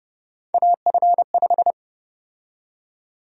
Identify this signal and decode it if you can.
Morse code, heard as a keyed tone, about 30 wpm, 700 Hz. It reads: AF5
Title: USA Call Sign Prefixes - Repeated 2x - 30wpm